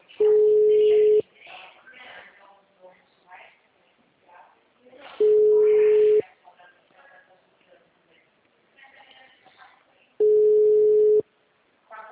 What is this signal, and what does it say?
Ringback tone (Europe, 425 Hz), with speech.